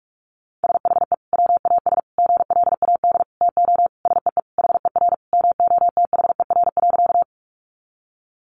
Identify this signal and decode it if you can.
Morse code, heard as a keyed tone, about 44 wpm, 700 Hz. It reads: SHE WAS GLAD TO SEE HER MOTHER.
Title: Sentences from Top 800 Words 45wpm